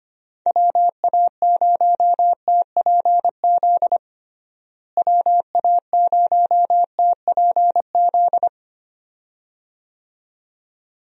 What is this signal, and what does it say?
Morse code, heard as a keyed tone, about 25 wpm, 700 Hz. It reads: WA0TPZ WA0TPZ
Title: USA Call Signs - 6 Characters - Repeated 2x - 25wpm